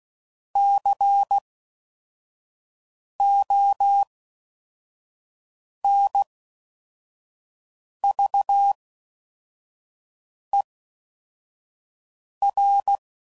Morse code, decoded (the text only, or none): CONVER